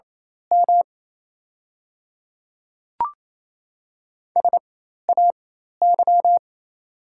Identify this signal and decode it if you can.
Morse code, heard as a keyed tone, about 28 wpm, spaced as for 13 wpm, 690 Hz. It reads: M SAY
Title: 4x Character Spacing - Top 200 Words - 28wpm